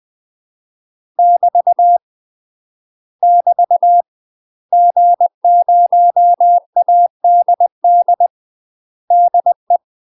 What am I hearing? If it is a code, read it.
Morse code, heard as a keyed tone, about 20 wpm, 700 Hz. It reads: <BT> <BT> G0ADD DE
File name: UK MASTER QSOs TRAINING / 20WPM / UK MASTER QSOs 0025